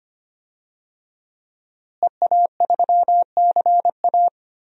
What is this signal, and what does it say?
Morse code, heard as a keyed tone, about 25 wpm, 700 Hz. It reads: EA3CA
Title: DXCC-100 Call Signs - 5 Characters - Repeated 3x - 25wpm